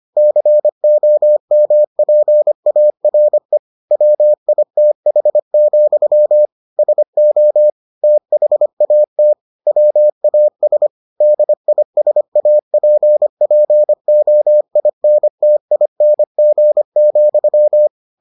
Morse code, heard as a keyed tone, about 25 wpm, 600 Hz. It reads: COMPARE WITH, SO THAT WAS DISAPPOINTING,